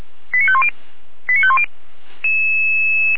Five-tone selective call (ZVEI), heard as quick runs of five tones, tones of about 70 ms. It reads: 89510 89510
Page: SDR/ZVEI